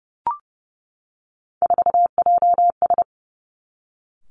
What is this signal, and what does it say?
Morse code, heard as a keyed tone, about 30 wpm, 700 Hz. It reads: 4JS
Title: Three Challenging Letter-Number Combinations 30wpm